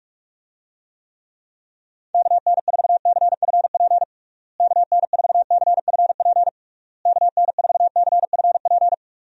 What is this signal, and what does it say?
Morse code, heard as a keyed tone, about 46 wpm, 700 Hz. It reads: KN4CFP KN4CFP KN4CFP
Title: USA Call Signs - 6 Characters - Repeated 3x - 45wpm